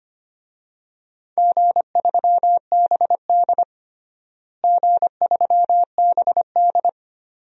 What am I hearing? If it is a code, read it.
Morse code, heard as a keyed tone, about 25 wpm, 700 Hz. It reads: G3BD G3BD
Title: DXCC-100 Call Signs - 4 Characters - Repeated 2x - 25wpm